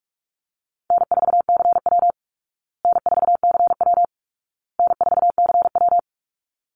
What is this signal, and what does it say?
Morse code, heard as a keyed tone, about 44 wpm, 700 Hz. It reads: N4CW N4CW N4CW
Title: Call Signs Repeated 2x - 45wpm